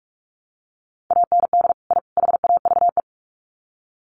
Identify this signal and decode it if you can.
Morse code, heard as a keyed tone, about 44 wpm, 700 Hz. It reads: AND I HAVE
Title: Sets of 3 Words 45wpm